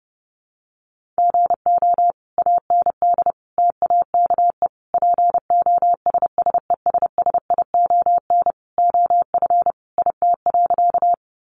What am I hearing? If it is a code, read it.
Morse code, heard as a keyed tone, about 30 wpm, 700 Hz. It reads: GO AND TAKE POSSESSION OF IT.
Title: Sentences from Top 1400 Words 30wpm